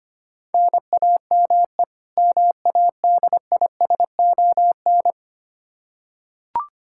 Morse code, heard as a keyed tone, about 25 wpm, 700 Hz. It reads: NAME MADISON
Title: Practice QSO - Exchange 1 - Beginning - 25wpm